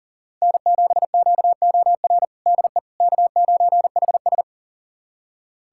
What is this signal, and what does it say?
Morse code, heard as a keyed tone, about 37 wpm, 700 Hz. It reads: N7QOR DE K9HS
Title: USA Callsign DE Callsign 40wpm